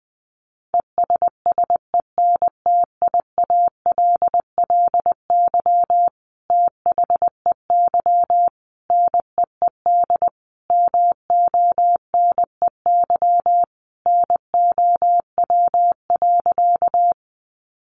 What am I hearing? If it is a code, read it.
Morse code, heard as a keyed tone, about 20 wpm, 700 Hz. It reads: ESSENTIALLY THEY NEED MONEY NOW.
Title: Sentences from Top 1300 Words 20wpm